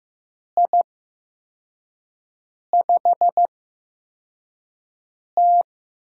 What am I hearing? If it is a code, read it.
Morse code, heard as a keyed tone, about 15 wpm, spaced as for 4 wpm, 700 Hz. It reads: I5T
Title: USA Call Signs - 8x Character Spacing - 15wpm